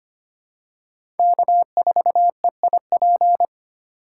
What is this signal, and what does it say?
Morse code, heard as a keyed tone, about 25 wpm, 700 Hz. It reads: K4EIP
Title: USA Call Signs - 5 Characters - 3x Character Spacing - 25wpm